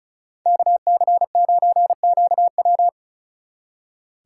Morse code, heard as a keyed tone, about 43 wpm, 700 Hz. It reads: KC9QW
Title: Speed Racing - USA 2x2 Call Signs 35wpm